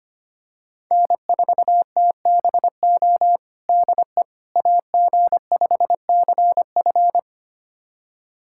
Morse code, heard as a keyed tone, about 25 wpm, 700 Hz. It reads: N4TBO DE AG5CF
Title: USA Callsign DE Callsign - 5 Characters - 25wpm